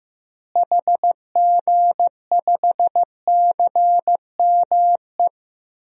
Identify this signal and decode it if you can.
Morse code, heard as a keyed tone, about 15 wpm, 700 Hz. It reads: HG5CME
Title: DXCC-100 Call Signs - 6 Characters - Repeated 3x - 15wpm